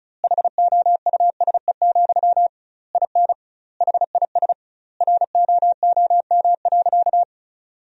Morse code, heard as a keyed tone, about 34 wpm, 700 Hz. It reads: HOUSE, IN HIS ROOM.
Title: 2x Word-Spacing - Sentences from Top 300 Words - 35wpm